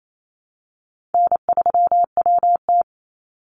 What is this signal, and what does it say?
Morse code, heard as a keyed tone, about 28 wpm, 700 Hz. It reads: N3WT
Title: Easy Call Signs Repeated 3x - 28wpm